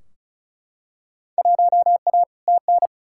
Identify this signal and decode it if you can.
Morse code, heard as a keyed tone, about 36 wpm, 700 Hz. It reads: 1A TN
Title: Field Day Exchanges - 35wpm